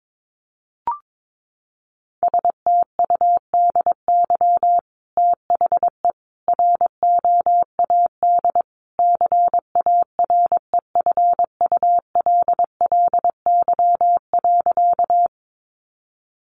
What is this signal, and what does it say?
Morse code, heard as a keyed tone, about 22 wpm, 700 Hz. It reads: STUDY THE ROAD CAREFULLY.
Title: Sentences from Top 1300 Words 22wpm